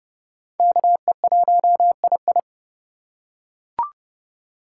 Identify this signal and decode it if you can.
Morse code, heard as a keyed tone, about 30 wpm, 700 Hz. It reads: KE1II
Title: USA Call Signs - 4x Character Spacing - 30wpm